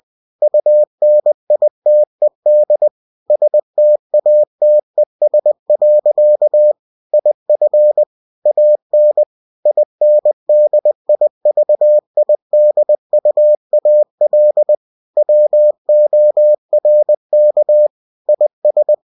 Morse code, heard as a keyed tone, about 20 wpm, 600 Hz. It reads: UNITED STATES. IF AN INDIVIDUAL WORK IS
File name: cw-books / winnie-the-pooh / winnie-20wpm-page-0114